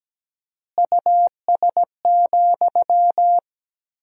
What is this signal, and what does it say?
Morse code, heard as a keyed tone, about 17 wpm, 700 Hz. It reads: US,
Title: 4x Word-Spacing - Sentences from Top 400 Words - 17wpm